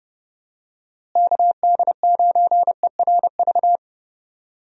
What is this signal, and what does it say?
Morse code, heard as a keyed tone, about 32 wpm, 700 Hz. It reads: KD9ERV
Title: USA Call Signs - 6 Characters - Repeated 3x - 30wpm